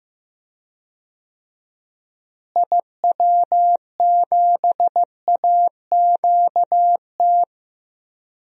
Morse code, heard as a keyed tone, about 15 wpm, 700 Hz. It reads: IW7AQT
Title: DXCC-100 Call Signs - 6 Characters - Repeated 2x - 15wpm